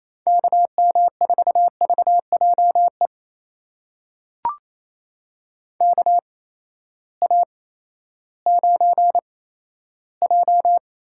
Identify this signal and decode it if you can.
Morse code, heard as a keyed tone, about 28 wpm, spaced as for 14 wpm, 700 Hz. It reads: KM4VJE KA9J
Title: USA Call Signs - 6 Characters - 8x Character Spacing - 28wpm